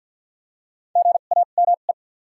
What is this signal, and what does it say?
Morse code, heard as a keyed tone, about 48 wpm, spaced as for 35 wpm, 700 Hz. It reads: GAME
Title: 2x Character Spacing - Top 500 Words - 50wpm